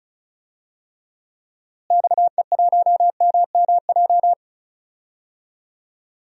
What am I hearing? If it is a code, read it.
Morse code, heard as a keyed tone, about 36 wpm, 700 Hz. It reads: XE1MMJ
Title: DXCC-100 Call Signs - 6 Characters - Repeated 2x - 35wpm